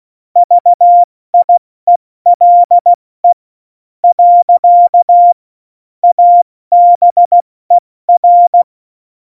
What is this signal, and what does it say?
Morse code, heard as a keyed tone, about 16 wpm, 700 Hz. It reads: VIELE . ABER
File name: Morsekurs / Witze / Witz 13 March 2018, 70 F80